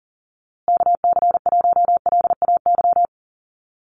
Morse code, heard as a keyed tone, about 40 wpm, 700 Hz. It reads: KC1LAY
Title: USA Call Signs Repeated 2x 40wpm